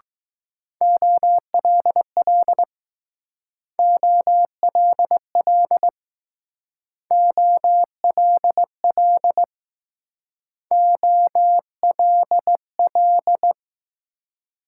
Morse code, heard as a keyed tone, about 20 wpm, 700 Hz. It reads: OLL OLL OLL OLL